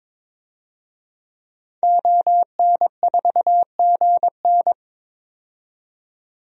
Morse code, heard as a keyed tone, about 22 wpm, 700 Hz. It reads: ON4GN
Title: DXCC-100 Call Signs - 5 Characters - 22wpm